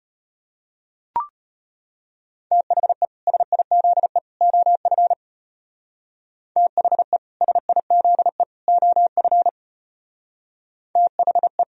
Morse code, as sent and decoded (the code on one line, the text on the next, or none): - .... / ... .. --.. / --- ..-. / - .... / ... .. --.. / --- ..-. / - .... .
THE SIZE OF THE SIZE OF THE